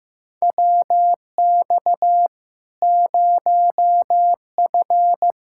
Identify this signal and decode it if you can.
Morse code, heard as a keyed tone, about 15 wpm, 700 Hz. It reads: WX 0F
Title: Practice QSO - Exchange 2 - Info - 15wpm